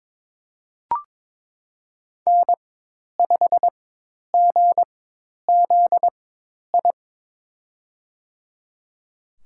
Morse code, heard as a keyed tone, about 22 wpm, spaced as for 10 wpm, 700 Hz. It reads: N5GZI
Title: USA Call Signs - 5 Characters - 4x Character Spacing - 22wpm